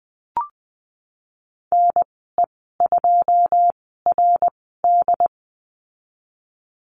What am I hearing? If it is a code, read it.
Morse code, heard as a keyed tone, about 20 wpm, spaced as for 15 wpm, 700 Hz. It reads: NE2RD
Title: 082 Lesson 14g - Number 2 - Call Signs 20wpm